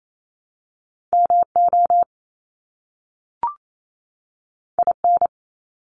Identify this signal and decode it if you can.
Morse code, heard as a keyed tone, about 28 wpm, 700 Hz. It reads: MO IN
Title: US States 28wpm